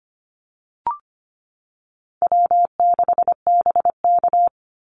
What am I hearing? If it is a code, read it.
Morse code, heard as a keyed tone, about 25 wpm, 700 Hz. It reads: W6BK
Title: Easy Call Signs Repeated 3x - 25wpm